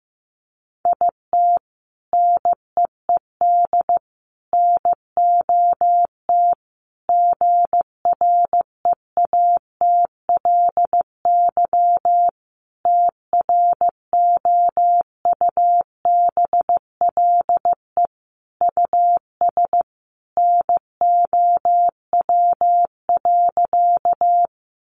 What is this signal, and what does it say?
Morse code, heard as a keyed tone, about 14 wpm, 700 Hz. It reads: IT NEED NOT GREATLY TROUBLE US NOW.
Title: Sentences from Top 1400 Words 15wpm